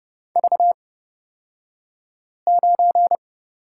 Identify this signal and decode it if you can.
Morse code, heard as a keyed tone, about 30 wpm, spaced as for 5 wpm, 700 Hz. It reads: V9
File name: Lesson 37d - Letter x - Two Characters 30wpm